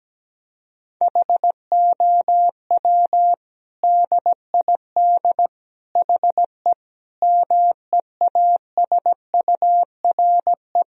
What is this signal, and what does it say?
Morse code, heard as a keyed tone, about 17 wpm, 700 Hz. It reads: HOW DID HE MEASURE
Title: Sentences from Top 1000 Words - Review Part 2 - 17wpm